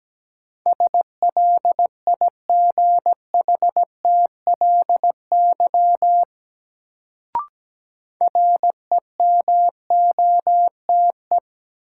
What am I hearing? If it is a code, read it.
Morse code, heard as a keyed tone, about 17 wpm, 700 Hz. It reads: SLIGHTLY REMOTE
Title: Sentences from Top 1300 Words - New Word Drill - 17wpm